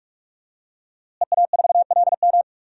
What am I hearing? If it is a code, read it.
Morse code, heard as a keyed tone, about 46 wpm, 700 Hz. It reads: EA4LM
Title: DXCC-100 Call Signs - 5 Characters - Repeated 3x - 45wpm